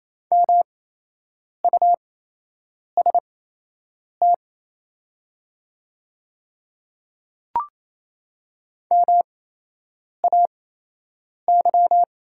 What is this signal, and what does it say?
Morse code, heard as a keyed tone, about 28 wpm, spaced as for 8 wpm, 700 Hz. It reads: MUST MAY
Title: 8x Character Spacing - Top 300 Words - 28wpm